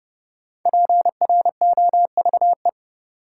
Morse code, heard as a keyed tone, about 30 wpm, 700 Hz. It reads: PROVE
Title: Sentences from Top 1600 Words - New Word Drill - 30wpm